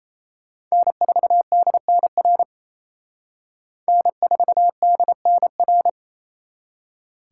Morse code, heard as a keyed tone, about 35 wpm, 700 Hz. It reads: N4DNR N4DNR